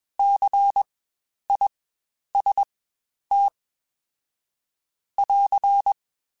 Morse code, heard as a keyed tone, about 20 wpm, spaced as for 10 wpm, 800 Hz. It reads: CIST <AR>
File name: Spaced Alphas / Spaced Alphas mp3s / Cs sp @20